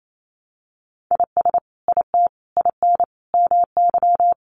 Morse code, heard as a keyed tone, about 28 wpm, 700 Hz. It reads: IS IT IN MY
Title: Sentences from Top 1200 Words 28wpm